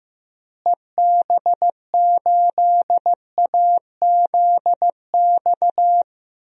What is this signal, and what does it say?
Morse code, heard as a keyed tone, about 15 wpm, 700 Hz. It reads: EB8AZX
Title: DXCC-100 Call Signs - 6 Characters - Repeated 2x - 15wpm